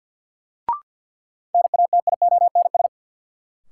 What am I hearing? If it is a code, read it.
Morse code, heard as a keyed tone, about 46 wpm, 700 Hz. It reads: NATIONS
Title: Sets of 3 Words - New Word Drill - 50wpm